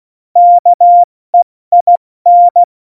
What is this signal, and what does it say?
Morse code, heard as a keyed tone, about 16 wpm, 700 Hz. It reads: KEIN
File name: Morsekurs / Witze / Witz 12. Februar 2019, 70 F80